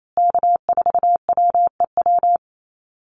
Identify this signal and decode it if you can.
Morse code, heard as a keyed tone, about 28 wpm, 700 Hz. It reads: K4WEW
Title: USA Call Signs Repeated 2x 28wpm